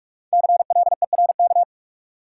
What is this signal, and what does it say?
Morse code, heard as a keyed tone, about 46 wpm, 700 Hz. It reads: CLERK
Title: Five Letter Words - 45wpm